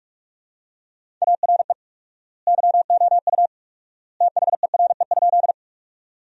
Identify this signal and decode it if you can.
Morse code, heard as a keyed tone, about 41 wpm, 700 Hz. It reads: ARE YOU THERE?
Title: 4x Word-Spacing - Sentences from Top 100 Words - 45wpm